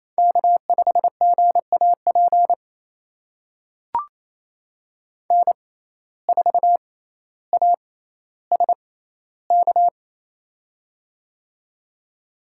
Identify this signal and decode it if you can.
Morse code, heard as a keyed tone, about 27 wpm, spaced as for 14 wpm, 700 Hz. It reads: K5GAP N4ASK
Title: USA Call Signs - 5 Characters - 6x Character Spacing - 28wpm